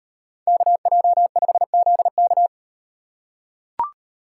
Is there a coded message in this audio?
Yes: Morse code from 0.5 to 2.5 s, heard as a keyed tone, about 38 wpm, 700 Hz. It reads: KJ5ZK